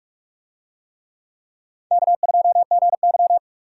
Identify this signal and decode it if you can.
Morse code, heard as a keyed tone, about 43 wpm, 700 Hz. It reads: K2GY